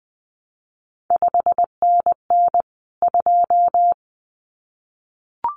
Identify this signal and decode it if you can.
Morse code, heard as a keyed tone, about 20 wpm, 700 Hz. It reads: 5NN 2
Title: CQ WW DX Contest Exchange - 20wpm